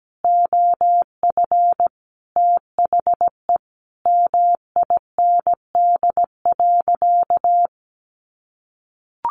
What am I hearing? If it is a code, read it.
Morse code, heard as a keyed tone, about 17 wpm, 700 Hz. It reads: OF THE MIND.
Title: Sentences from Top 900 Words 17wpm